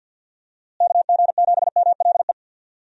Morse code, heard as a keyed tone, about 47 wpm, 700 Hz. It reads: KG7MLE
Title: USA Call Signs - 6 Characters - Repeated 2x - 50wpm